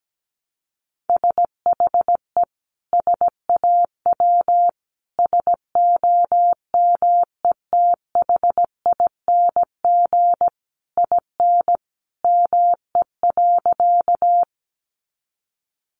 Morse code, heard as a keyed tone, about 17 wpm, 700 Hz. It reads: SHE SAW SOMETHING IN ME.
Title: Sentences from Top 600 Words 17wpm